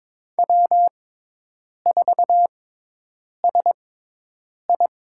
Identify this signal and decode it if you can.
Morse code, heard as a keyed tone, about 22 wpm, spaced as for 8 wpm, 700 Hz. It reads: W4SI
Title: USA Call Signs - 4 Characters - 6x Character Spacing - 22wpm